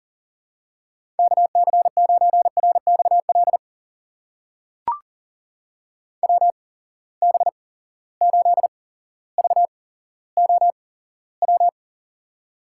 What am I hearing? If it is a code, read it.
Morse code, heard as a keyed tone, about 40 wpm, spaced as for 17 wpm, 700 Hz. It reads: KC9RXL WB8VOW
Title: USA Call Signs - 6 Characters - 8x Character Spacing - 40wpm